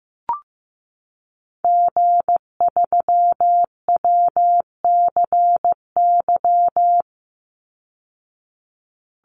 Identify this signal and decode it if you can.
Morse code, heard as a keyed tone, about 15 wpm, 700 Hz. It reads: G3WCY
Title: Call Signs Repeated 3x - 15wpm